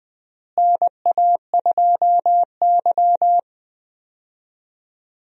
Morse code, heard as a keyed tone, about 20 wpm, 700 Hz. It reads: NA2Y
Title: USA Call Signs - 4 Characters - Repeated 3x - 20wpm